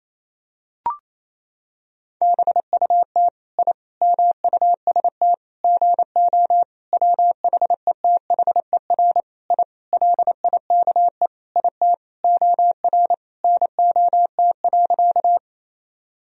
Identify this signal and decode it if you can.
Morse code, heard as a keyed tone, about 28 wpm, 700 Hz. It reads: BUT I MUST GO WHETHER I LIKE IT OR NOT.
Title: Sentences from Top 1000 Words - Review Part 3 - 28wpm